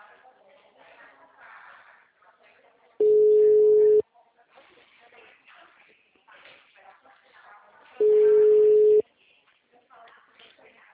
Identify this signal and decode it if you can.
Ringback tone (Europe, 425 Hz), with speech.